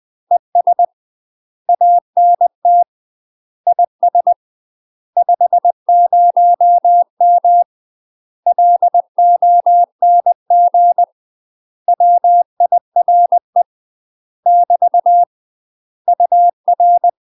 Morse code, heard as a keyed tone, about 20 wpm, 700 Hz. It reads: ES ANT IS 50M LONG WIRE <BT> UR
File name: UK MASTER QSOs TRAINING/20WPM